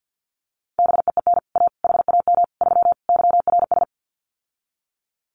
Morse code, heard as a keyed tone, about 50 wpm, 700 Hz. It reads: BEEN A HAM 3 YRS
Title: Common QSO Phrases Encore 50wpm